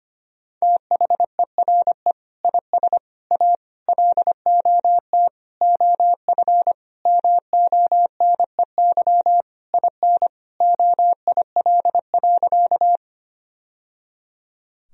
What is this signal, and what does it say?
Morse code, heard as a keyed tone, about 25 wpm, 700 Hz. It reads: THERE IS A LOT OF MONEY IN OIL.
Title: Sentences from Top 1000 Words - Review Part 4 - 25wpm